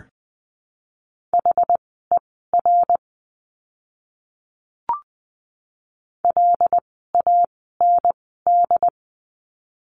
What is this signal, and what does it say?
Morse code, heard as a keyed tone, about 20 wpm, spaced as for 14 wpm, 700 Hz. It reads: HER LAND
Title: 101 Lesson 17f - Letter u - Words 20wpm